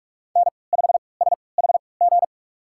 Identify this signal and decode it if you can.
Morse code, heard as a keyed tone, about 42 wpm, spaced as for 25 wpm, 700 Hz. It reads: N5SHG
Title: USA Call Signs - 5 Characters - 3x Character Spacing - 45wpm